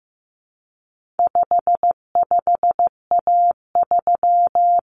Morse code, heard as a keyed tone, about 15 wpm, 700 Hz. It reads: 55A3
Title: Common Cut Numbers - 4-Digit 15wpm